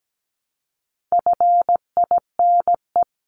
Morse code, heard as a keyed tone, about 17 wpm, 700 Hz. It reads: FINE
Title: Top 500 Words Repeated 17wpm